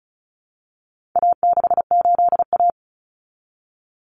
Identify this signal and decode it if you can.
Morse code, heard as a keyed tone, about 34 wpm, 700 Hz. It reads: A68A
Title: Common Cut Numbers - 4-Digit 35wpm